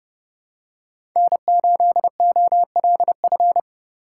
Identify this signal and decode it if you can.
Morse code, heard as a keyed tone, about 30 wpm, 700 Hz. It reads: N8OLF
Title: USA Call Signs - 5 Characters - 3x Character Spacing - 30wpm